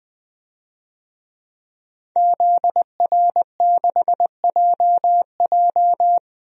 Morse code, heard as a keyed tone, about 21 wpm, 700 Hz. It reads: ZR6JJ
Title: DXCC-100 Call Signs - 5 Characters - 20wpm